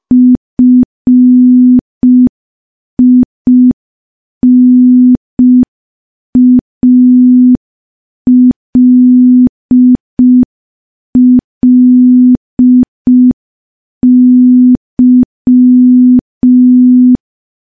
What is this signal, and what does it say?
Morse code, heard as a keyed tone, about 5 wpm, 250 Hz. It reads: FINALLY
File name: morse / morse-training-05WPM